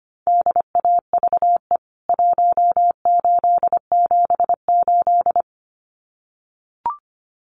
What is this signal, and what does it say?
Morse code, heard as a keyed tone, about 25 wpm, 700 Hz. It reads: DAVE 1878
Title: CWT Exchange 25wpm